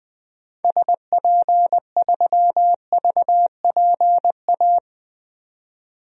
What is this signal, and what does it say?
Morse code, heard as a keyed tone, about 20 wpm, 700 Hz. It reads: SP3VPA
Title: DXCC-100 Call Signs - 6 Characters - 20wpm